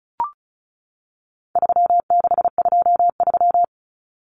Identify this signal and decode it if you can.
Morse code, heard as a keyed tone, about 34 wpm, 700 Hz. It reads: 3623
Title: Numbers - 4-Digit 35wpm